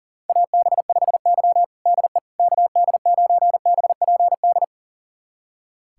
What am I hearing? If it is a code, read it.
Morse code, heard as a keyed tone, about 40 wpm, 700 Hz. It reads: AB5Y DE KD9BPD
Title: USA Callsign DE Callsign 40wpm